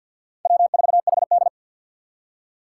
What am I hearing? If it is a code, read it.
Morse code, heard as a keyed tone, about 50 wpm, 700 Hz. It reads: W4HD
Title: Four Challenging Letter-Number Combinations 50wpm